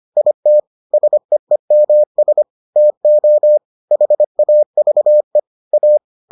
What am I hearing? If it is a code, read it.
Morse code, heard as a keyed tone, about 25 wpm, 600 Hz. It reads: IT SEEMS TO HAVE A